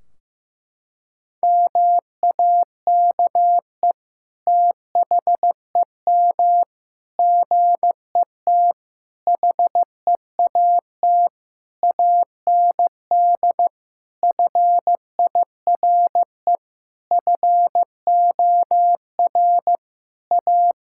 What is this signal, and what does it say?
Morse code, heard as a keyed tone, about 16 wpm, 700 Hz. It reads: MAKE THEM GET HEAT AND FIRE FOR A
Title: Sentences from Top 1600 Words 15wpm